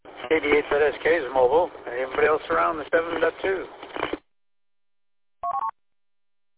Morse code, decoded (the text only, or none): none